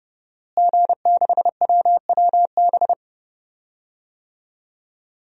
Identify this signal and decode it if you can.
Morse code, heard as a keyed tone, about 32 wpm, 700 Hz. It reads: G6WWB
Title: DXCC-100 Call Signs - 5 Characters - Repeated 2x - 30wpm